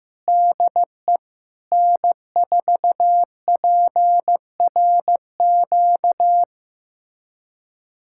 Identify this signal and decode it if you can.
Morse code, heard as a keyed tone, about 15 wpm, 700 Hz. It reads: DE N4PRQ
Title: USA Callsign DE Callsign - 5 Characters - 15wpm